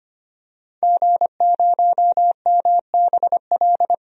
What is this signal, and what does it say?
Morse code, heard as a keyed tone, about 25 wpm, 700 Hz. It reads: G0MBL
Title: DXCC-100 Call Signs - 5 Characters - 25wpm